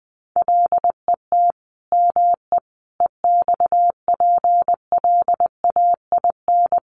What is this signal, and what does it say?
Morse code, heard as a keyed tone, about 20 wpm, 700 Hz. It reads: LET ME EXPLAIN
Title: Sentences from Top 1100 Words 20wpm